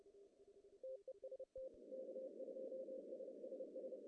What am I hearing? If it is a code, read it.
Morse code, heard as a keyed tone, about 30 wpm, 520 Hz, with noise. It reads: TEST K2SSS
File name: CW / cqww-cw-k5tr